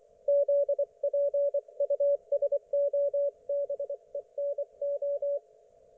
Morse code, heard as a keyed tone, about 24 wpm, 570 Hz. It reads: ZPUSOBENO
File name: Archiv / CUC224 / ok1fhi cul224